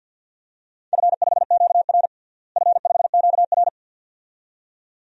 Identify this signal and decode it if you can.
Morse code, heard as a keyed tone, about 51 wpm, 700 Hz. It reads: F5QR F5QR
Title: DXCC-100 Call Signs - 4 Characters - Repeated 2x - 50wpm